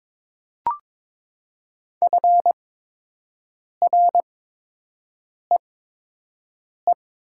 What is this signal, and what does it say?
Morse code, heard as a keyed tone, about 22 wpm, spaced as for 6 wpm, 700 Hz. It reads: FREE
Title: 8x Character Spacing - Top 500 Words - 22wpm